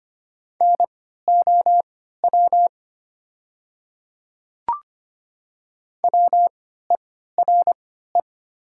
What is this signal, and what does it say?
Morse code, heard as a keyed tone, about 25 wpm, spaced as for 14 wpm, 700 Hz. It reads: NOW WERE
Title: 3x Character Spacing - Top 100 Words - 25wpm